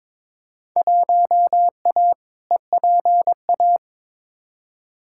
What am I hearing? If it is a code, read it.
Morse code, heard as a keyed tone, about 21 wpm, 700 Hz. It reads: 1A EPA
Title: Field Day Exchanges - 22wpm